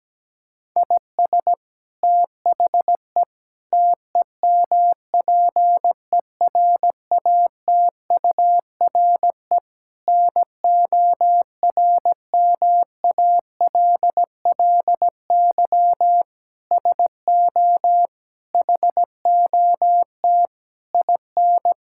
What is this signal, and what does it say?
Morse code, heard as a keyed tone, about 17 wpm, 700 Hz. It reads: IS THE TEMPERATURE NORMALLY SO HOT IN
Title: Sentences from Top 1000 Words - Review Part 6 - 17wpm